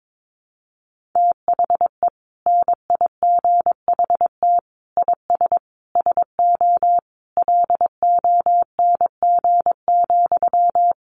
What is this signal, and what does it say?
Morse code, heard as a keyed tone, about 22 wpm, 700 Hz. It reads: THE NIGHT IS SO LONG,
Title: Sentences from Top 500 Words 22wpm